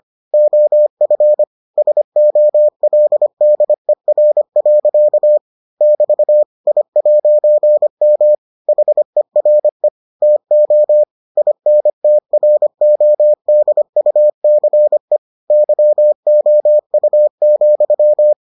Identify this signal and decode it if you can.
Morse code, heard as a keyed tone, about 25 wpm, 600 Hz. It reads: OF SOLDER. <BT> I'M HERE TO INTRODUCE YOU,